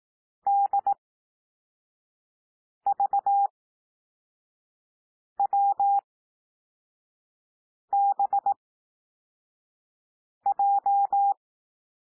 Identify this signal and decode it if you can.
Morse code, heard as a keyed tone, about 18 wpm, spaced as for 4 wpm, 800 Hz. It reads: DVWBJ
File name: cwmp3 / L2 SOUND LETTERS 18 wpm